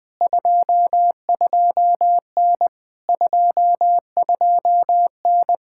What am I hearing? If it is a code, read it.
Morse code, heard as a keyed tone, about 20 wpm, 700 Hz. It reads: 22N 22N
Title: POTA QSO Practice - 20wpm